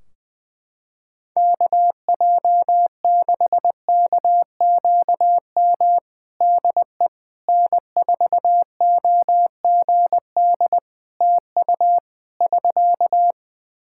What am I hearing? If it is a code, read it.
Morse code, heard as a keyed tone, about 20 wpm, 700 Hz. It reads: KJ6KQM DE N4OGD TU <SK>
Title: Practice QSO - Exchange 4 - Ending - 20wpm